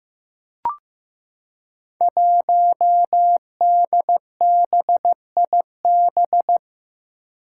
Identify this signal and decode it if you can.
Morse code, heard as a keyed tone, about 15 wpm, 700 Hz. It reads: 1DBIB
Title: Five Challenging Letter-Number Combinations 15wpm